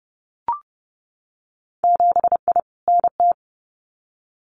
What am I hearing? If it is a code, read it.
Morse code, heard as a keyed tone, about 30 wpm, 700 Hz. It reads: 7I NT
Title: Winter Field Day Exchanges Encore 30wpm